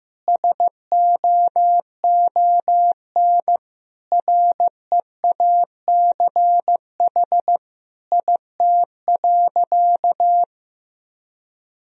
Morse code, heard as a keyed tone, about 15 wpm, 700 Hz. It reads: SOON REACH IT.
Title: Sentences from Top 400 Words and Abbreviations 15wpm